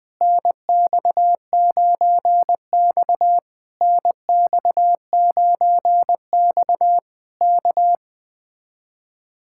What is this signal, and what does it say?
Morse code, heard as a keyed tone, about 20 wpm, 700 Hz. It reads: NX9X NX9X K